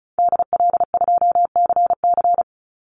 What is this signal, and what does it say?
Morse code, heard as a keyed tone, about 34 wpm, 700 Hz. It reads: DL2CC
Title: Call Signs Repeated 2x - 35wpm